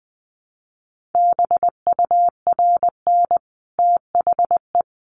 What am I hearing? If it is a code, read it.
Morse code, heard as a keyed tone, about 20 wpm, 700 Hz. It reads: BURN THE
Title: Sentences from Top 1300 Words 20wpm